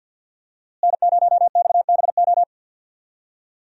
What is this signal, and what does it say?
Morse code, heard as a keyed tone, about 49 wpm, 700 Hz. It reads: N0XBO
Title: USA Call Signs - 5 Characters - Repeated 2x - 50wpm